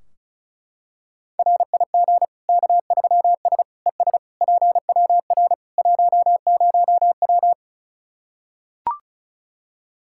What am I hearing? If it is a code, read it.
Morse code, heard as a keyed tone, about 34 wpm, 700 Hz. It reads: RIG K3S ES PWR 10W
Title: Practice QSO - Exchange 2 - Info - 35wpm